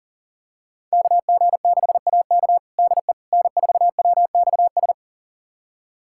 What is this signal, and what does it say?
Morse code, heard as a keyed tone, about 37 wpm, 700 Hz. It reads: KG6AK DE N4WXS
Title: USA Callsign DE Callsign - 5 Characters - 40wpm